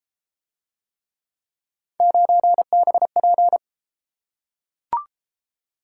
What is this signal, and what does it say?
Morse code, heard as a keyed tone, about 32 wpm, 700 Hz. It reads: 9BP